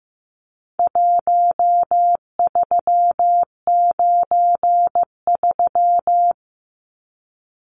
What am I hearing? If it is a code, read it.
Morse code, heard as a keyed tone, about 15 wpm, 700 Hz. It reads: 1393
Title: Numbers - 4-Digit 15wpm